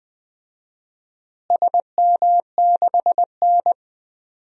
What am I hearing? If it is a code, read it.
Morse code, heard as a keyed tone, about 20 wpm, 700 Hz. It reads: SM6N
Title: DXCC-100 Call Signs - 4 Characters - Repeated 3x - 20wpm